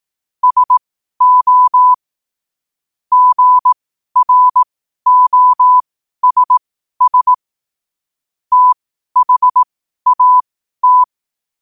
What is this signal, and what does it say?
Morse code, heard as a keyed tone, about 18 wpm, spaced as for 12 wpm, 1000 Hz. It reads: SO GROSS THAT
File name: qotdfiles / 12wpm